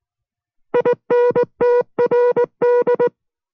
Morse code, heard as a keyed tone, about 19 wpm, 490 Hz. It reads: INTRD